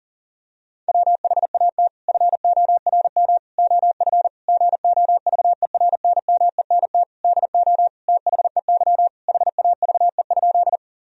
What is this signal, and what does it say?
Morse code, heard as a keyed tone, about 40 wpm, 700 Hz. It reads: WHAT FORM OF GOVERNMENT DO THEY HAVE?
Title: Sentences from Top 1000 Words - Review Part 3 - 40wpm